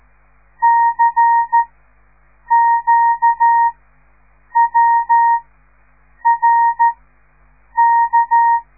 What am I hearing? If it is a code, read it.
Morse code, heard as a keyed tone, about 13 wpm, spaced as for 7 wpm, 930 Hz. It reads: CQWRK
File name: radiomorningstar / morse / les 5